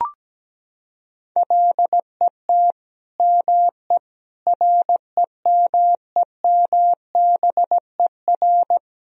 Morse code, heard as a keyed tone, about 17 wpm, 700 Hz. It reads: LET ME REMEMBER
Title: Sentences from Top 400 Words and Abbreviations 17wpm